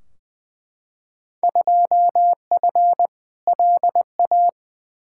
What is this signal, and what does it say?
Morse code, heard as a keyed tone, about 21 wpm, 700 Hz. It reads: 2F LA
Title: Field Day Exchanges - 20wpm